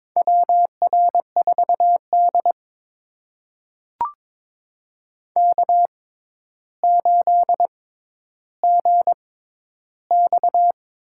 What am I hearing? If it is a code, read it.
Morse code, heard as a keyed tone, about 22 wpm, spaced as for 11 wpm, 700 Hz. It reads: WR4D K8GX
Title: USA Call Signs - 4 Characters - 6x Character Spacing - 22wpm